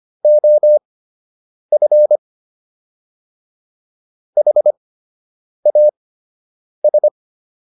Morse code, heard as a keyed tone, about 25 wpm, spaced as for 8 wpm, 600 Hz. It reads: OF HAS